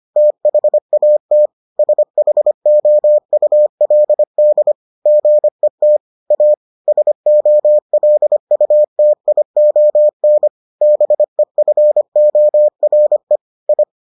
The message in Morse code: THAT SHOULD GET A SOLUTION BEFORE I

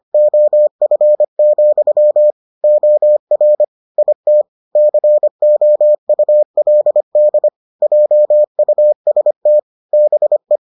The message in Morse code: OF, OR IT COULD JUST BE

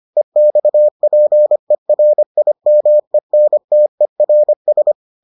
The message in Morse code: EXPERIMENTERS

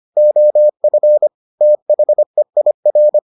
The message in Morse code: OF THEIR

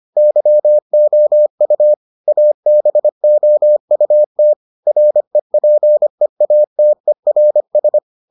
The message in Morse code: YOU ABOUT REPEATERS

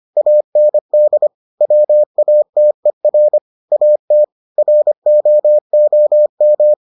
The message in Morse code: AND WATER AT ROOM

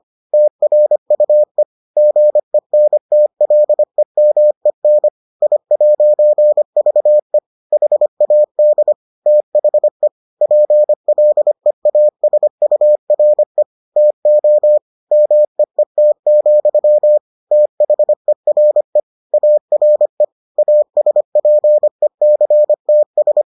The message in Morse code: TRUE GENTLEMEN I'VE HAD THE PLEASURE TO MEET, THERE ARE ASPECTS

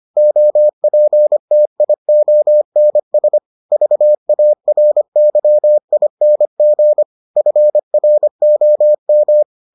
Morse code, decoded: OPTIONS VARYING FROM

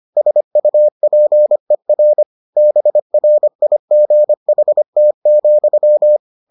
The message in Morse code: SUPER BRIGHT,

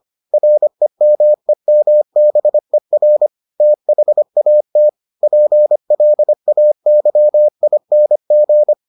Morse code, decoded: REMEMBER THAT PLAYING

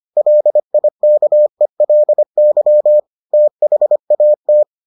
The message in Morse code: LIKELY THAT

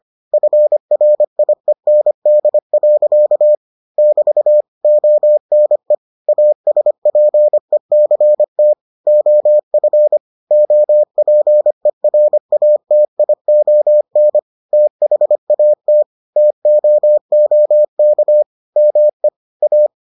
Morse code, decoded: FRIEND. <BT> ONE ASPECT OF OPERATION THAT TOOK ME A

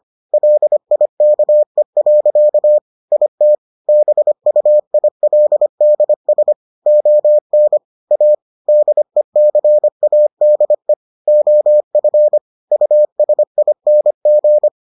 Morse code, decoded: LIKE. IT BUILDS ON A DECADE OF USING

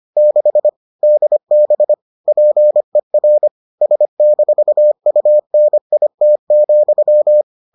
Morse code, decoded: 6 DB PER S-UNIT,

